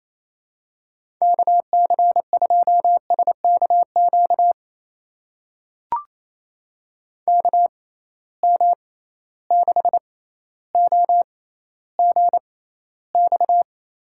Morse code, heard as a keyed tone, about 28 wpm, spaced as for 14 wpm, 700 Hz. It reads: KC2SKQ KM6OGX
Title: USA Call Signs - 6 Characters - 6x Character Spacing - 28wpm